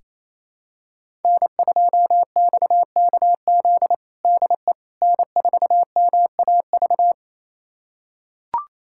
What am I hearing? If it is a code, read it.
Morse code, heard as a keyed tone, about 28 wpm, 700 Hz. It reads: N2XKZ DE N4MAV
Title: USA Callsign DE Callsign - 5 Characters - 28wpm